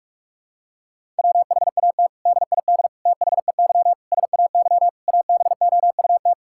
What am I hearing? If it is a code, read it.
Morse code, heard as a keyed tone, about 45 wpm, 700 Hz. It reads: WHAT DID THEY SAY ABOUT
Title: Sentences from Top 1000 Words - Review Part 5 - 45wpm